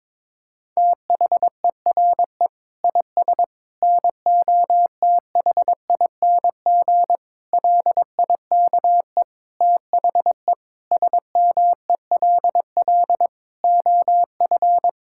Morse code, heard as a keyed tone, about 22 wpm, 700 Hz. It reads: THERE IS NOTHING LIKE THE SMELL OF
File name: Difficult Sentences with Hints - 22wpm